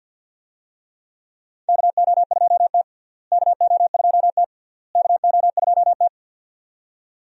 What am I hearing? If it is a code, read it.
Morse code, heard as a keyed tone, about 52 wpm, 700 Hz. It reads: KO2T KO2T KO2T
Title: USA Call Signs - 4 Characters - Repeated 3x - 50wpm